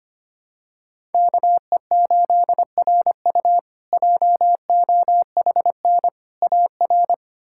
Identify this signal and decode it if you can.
Morse code, heard as a keyed tone, about 25 wpm, 700 Hz. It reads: KE8RU JOHN AR
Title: K1USN SST QSO Practice - 25wpm